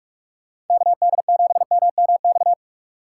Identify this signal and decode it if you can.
Morse code, heard as a keyed tone, about 46 wpm, 700 Hz. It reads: KD7MMX
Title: USA Call Signs - 2x Character Spacing - 45wpm